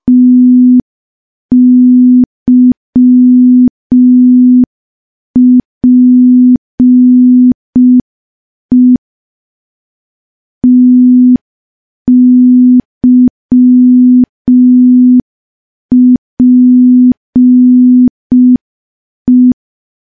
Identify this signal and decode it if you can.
Morse code, heard as a keyed tone, about 5 wpm, 250 Hz. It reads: TYPE TYPE
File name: morse / morse-training-05WPM